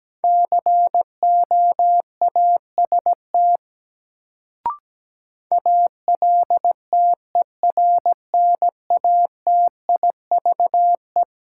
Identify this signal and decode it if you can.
Morse code, heard as a keyed tone, about 17 wpm, 700 Hz. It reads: COAST ALTERNATIVE
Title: Sentences from Top 1000 Words - Review Part 4 - New Word Drill - 17wpm